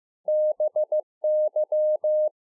Morse code, heard as a keyed tone, about 15 wpm, 610 Hz. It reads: BY